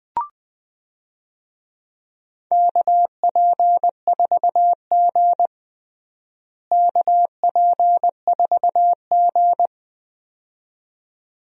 Morse code, heard as a keyed tone, about 20 wpm, 700 Hz. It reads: KP4G KP4G
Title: DXCC-100 Call Signs - 4 Characters - Repeated 2x - 20wpm